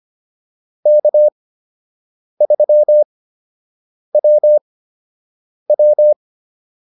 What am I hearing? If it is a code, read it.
Morse code, heard as a keyed tone, about 25 wpm, spaced as for 7 wpm, 600 Hz. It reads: K3WW